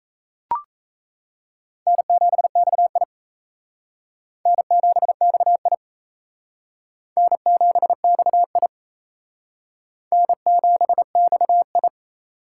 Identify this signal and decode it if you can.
Morse code, heard as a keyed tone, about 34 wpm, 700 Hz. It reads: N7XI N7XI N7XI N7XI
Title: Speed Racing - USA 1x2 Call Signs 28wpm